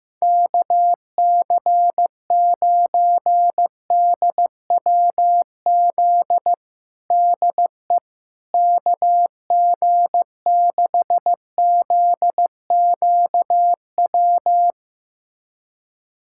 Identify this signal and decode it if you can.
Morse code, heard as a keyed tone, about 15 wpm, 700 Hz. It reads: KC9DWZ DE KG6ZQW
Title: USA Callsign DE Callsign - 6 Characters - 15wpm